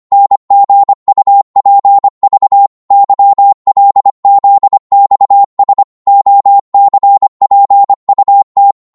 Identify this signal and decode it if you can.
Morse code, heard as a keyed tone, about 25 wpm, 800 Hz. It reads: NGUPV YLZXS OCPUT